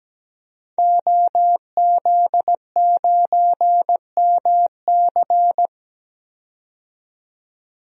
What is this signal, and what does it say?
Morse code, heard as a keyed tone, about 17 wpm, 700 Hz. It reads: OZ9MC
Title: DXCC-100 Call Signs - 5 Characters - 17wpm